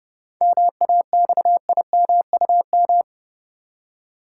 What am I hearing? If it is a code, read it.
Morse code, heard as a keyed tone, about 32 wpm, 700 Hz. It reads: MAXIMUM
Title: Seven Letter Words - 30wpm